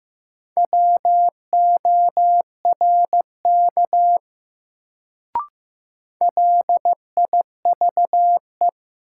Morse code, heard as a keyed tone, about 15 wpm, 700 Hz. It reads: WORK LIVE
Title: Sentences from Top 200 Words - New Word Drill - 15wpm